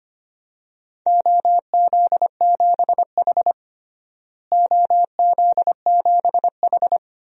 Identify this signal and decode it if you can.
Morse code, heard as a keyed tone, about 25 wpm, 700 Hz. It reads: OZ7H OZ7H
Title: DXCC-100 Call Signs - 4 Characters - Repeated 2x - 25wpm